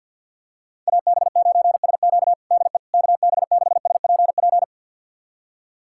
Morse code, heard as a keyed tone, about 51 wpm, 700 Hz. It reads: AB9SQ DE KB6SPP
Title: USA Callsign DE Callsign 50wpm